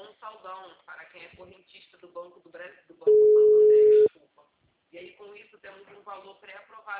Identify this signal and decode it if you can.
Ringback tone (Europe, 425 Hz), with speech.
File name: out- -2018- - - .30